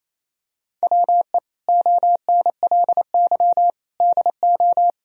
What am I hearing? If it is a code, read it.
Morse code, heard as a keyed tone, about 28 wpm, 700 Hz. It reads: WE ONLY DO
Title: Sentences from Top 1000 Words - Review Part 4 - 28wpm